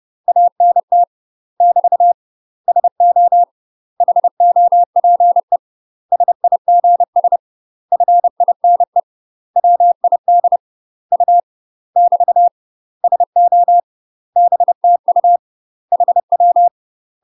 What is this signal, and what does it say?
Morse code, heard as a keyed tone, about 30 wpm, 700 Hz. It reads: ANT <BT> SO HOPE SIGS FINE WID U <BT> SO BTU HW
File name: UK MASTER QSOs TRAINING / 30WPM / UK MASTER QSOs 0027